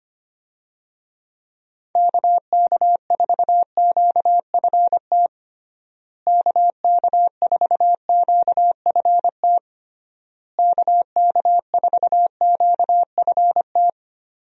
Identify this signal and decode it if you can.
Morse code, heard as a keyed tone, about 25 wpm, 700 Hz. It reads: KK4QFT KK4QFT KK4QFT
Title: USA Call Signs - 6 Characters - Repeated 3x - 25wpm